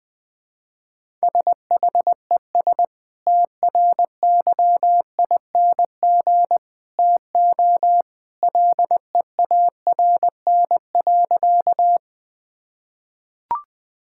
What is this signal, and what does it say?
Morse code, heard as a keyed tone, about 20 wpm, 700 Hz. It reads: SHES TRYING TO LEARN.
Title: Sentences with 2-Word Phrases - 20wpm